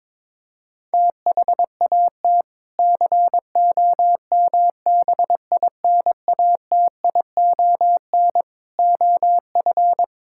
Morse code, heard as a keyed tone, about 22 wpm, 700 Hz. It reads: THAT COMBINATION OF
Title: Sentences from Top 1000 Words - Review Part 6 - 22wpm